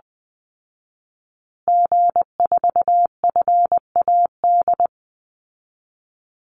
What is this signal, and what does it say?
Morse code, heard as a keyed tone, about 20 wpm, 700 Hz. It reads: G4FAD
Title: Call Signs Repeated 2x - 20wpm